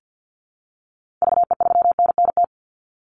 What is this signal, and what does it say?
Morse code, heard as a keyed tone, about 50 wpm, 700 Hz. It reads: VE3NNT
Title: Call Signs - 50wpm